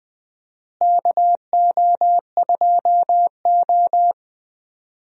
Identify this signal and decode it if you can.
Morse code, heard as a keyed tone, about 20 wpm, 700 Hz. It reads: KO2O
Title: USA Call Signs - 4 Characters - Repeated 3x - 20wpm